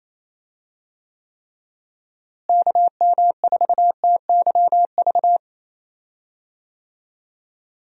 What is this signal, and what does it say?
Morse code, heard as a keyed tone, about 28 wpm, 700 Hz. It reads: KM4TYV
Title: USA Call Signs - 6 Characters - Repeated 3x - 28wpm